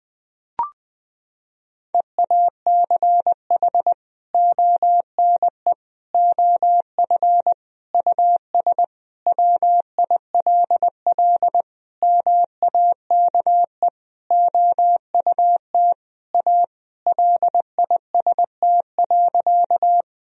Morse code, heard as a keyed tone, about 20 wpm, 700 Hz. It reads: EACH ONE OF US WILL MAKE OUT A LIST.
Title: Sentences from Top 1000 Words - Review Part 2 - 20wpm